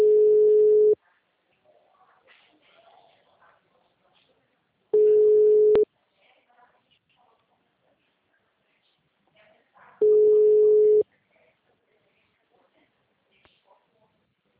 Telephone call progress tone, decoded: ringback tone

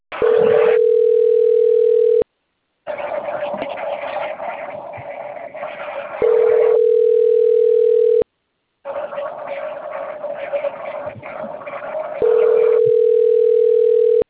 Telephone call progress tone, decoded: ringback tone

